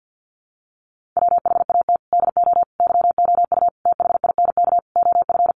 Morse code, heard as a keyed tone, about 50 wpm, 700 Hz. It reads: WHAT DO YOU THINK OF
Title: Sentences from Top 1500 Words 50wpm